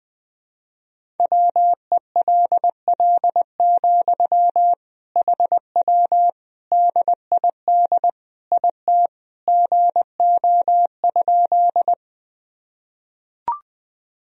Morse code, heard as a keyed tone, about 21 wpm, 700 Hz. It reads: WELL, HW DID IT GO?
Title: Sentences from Top 200 Words and Abbreviations 20wpm